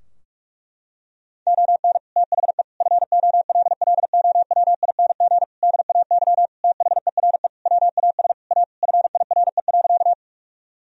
Morse code, heard as a keyed tone, about 47 wpm, 700 Hz. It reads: ON THE FOLLOWING DAY THERE WAS A FIRE.
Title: Sentences from Top 1000 Words - Review Part 4 - 45wpm